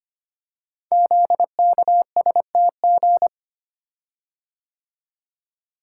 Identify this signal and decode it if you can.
Morse code, heard as a keyed tone, about 25 wpm, 690 Hz. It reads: ZKSTG
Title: Code Groups - 5 Letters - 25wpm